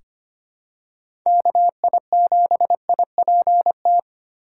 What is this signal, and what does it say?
Morse code, heard as a keyed tone, about 25 wpm, 700 Hz. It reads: KI7IPT
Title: USA Call Signs - 6 Characters - 3x Character Spacing - 25wpm